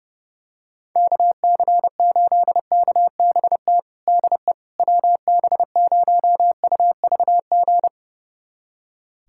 Morse code, heard as a keyed tone, about 30 wpm, 700 Hz. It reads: KC8KBT DE WB0UVG
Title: USA Callsign DE Callsign - 6 Characters - 30wpm